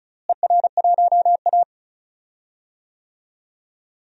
Morse code, heard as a keyed tone, about 34 wpm, 700 Hz. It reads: ER1A
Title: DXCC-100 Call Signs - 4 Characters - Repeated 3x - 35wpm